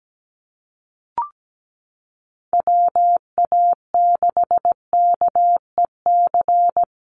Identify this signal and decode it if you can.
Morse code, heard as a keyed tone, about 17 wpm, 700 Hz. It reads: WA6KEC
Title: USA Call Signs Repeated 2x 17wpm